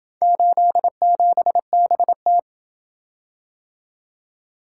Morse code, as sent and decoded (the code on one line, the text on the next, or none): ---.. --... -... -
87BT